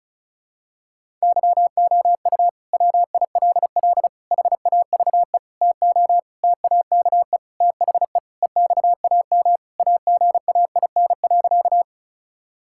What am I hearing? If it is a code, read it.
Morse code, heard as a keyed tone, about 36 wpm, 700 Hz. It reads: YOU WILL HAVE TO TAKE THE EXAM AGAIN.
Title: Sentences from Top 1000 Words - Review Part 5 - 35wpm